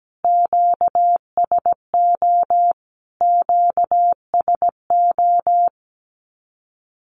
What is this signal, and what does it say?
Morse code, heard as a keyed tone, about 17 wpm, 700 Hz. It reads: QSO QSO
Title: Q-Codes Repeated 3x 17wpm